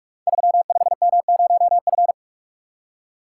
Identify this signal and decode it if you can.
Morse code, heard as a keyed tone, about 44 wpm, 700 Hz. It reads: 35M0F
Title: Code Groups - 5 Letter:Numbers - 45wpm